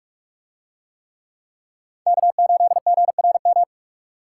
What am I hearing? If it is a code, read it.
Morse code, heard as a keyed tone, about 44 wpm, 700 Hz. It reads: K8GRM